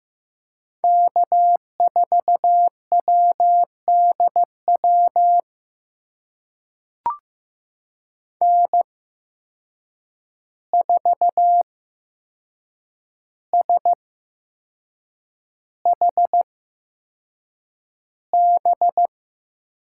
Morse code, heard as a keyed tone, about 15 wpm, spaced as for 6 wpm, 700 Hz. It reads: K4WDW N4SHB
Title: USA Call Signs - 5 Characters - 8x Character Spacing - 15wpm